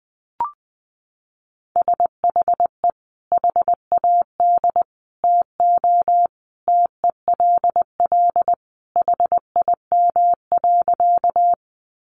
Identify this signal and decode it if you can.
Morse code, heard as a keyed tone, about 20 wpm, 700 Hz. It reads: SHE HAD TO TELL HIM.
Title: Sentences from Top 200 Words 20wpm